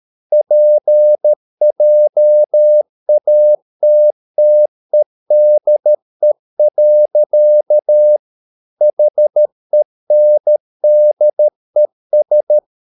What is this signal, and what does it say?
Morse code, heard as a keyed tone, about 13 wpm, 600 Hz. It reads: PJATTEDE. HENDES
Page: Ferie 13 wpm